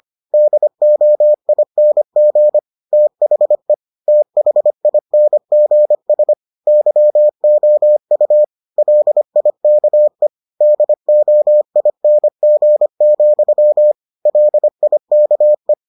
Morse code, heard as a keyed tone, about 25 wpm, 600 Hz. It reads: DOING THE THINGS YOU LIKE DOING, LIKE